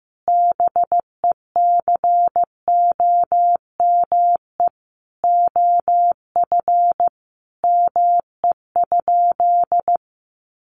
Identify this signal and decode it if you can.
Morse code, heard as a keyed tone, about 15 wpm, 700 Hz. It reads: BECOME OF ME?
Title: Sentences from Top 600 Words 15wpm